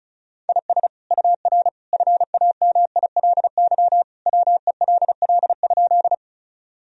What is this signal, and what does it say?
Morse code, heard as a keyed tone, about 36 wpm, 700 Hz. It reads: IS UR FAMILY WELL?
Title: Sentences from Top 400 Words and Abbreviations 35wpm